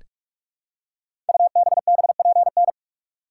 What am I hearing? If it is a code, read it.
Morse code, heard as a keyed tone, about 42 wpm, 700 Hz. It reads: UBBPN
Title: Code Groups - 5 Letter:Numbers - 45wpm